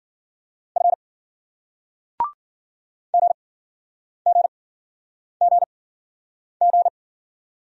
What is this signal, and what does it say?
Morse code, heard as a keyed tone, about 54 wpm, spaced as for 9 wpm, 700 Hz. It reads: F GGGG